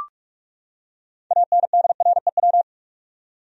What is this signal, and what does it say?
Morse code, heard as a keyed tone, about 41 wpm, 700 Hz. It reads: ANDREW
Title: USA Common Names - 45wpm